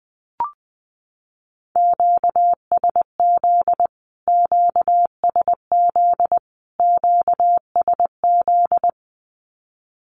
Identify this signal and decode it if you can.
Morse code, heard as a keyed tone, about 20 wpm, 700 Hz. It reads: QSZ QSZ QSZ
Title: Q-Codes Repeated 3x 20wpm